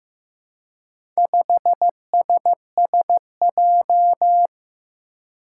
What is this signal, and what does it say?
Morse code, heard as a keyed tone, about 15 wpm, 700 Hz. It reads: 5SSJ
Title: Four Challenging Letter-Number Combinations 15wpm